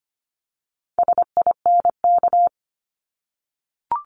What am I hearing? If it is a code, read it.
Morse code, heard as a keyed tone, about 25 wpm, 700 Hz. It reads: SINK
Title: Multi-letter Phonemes Repeated 3x 25wpm